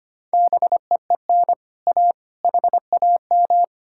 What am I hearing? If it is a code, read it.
Morse code, heard as a keyed tone, about 33 wpm, 700 Hz. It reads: BEEN A HAM